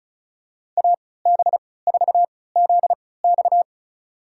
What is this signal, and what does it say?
Morse code, heard as a keyed tone, about 36 wpm, spaced as for 20 wpm, 700 Hz. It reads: AB4ZX
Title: USA Call Signs - 5 Characters - 3x Character Spacing - 35wpm